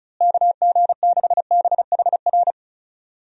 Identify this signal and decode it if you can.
Morse code, heard as a keyed tone, about 36 wpm, 700 Hz. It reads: KG6BHR
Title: USA Call Signs - 6 Characters - Repeated 3x - 35wpm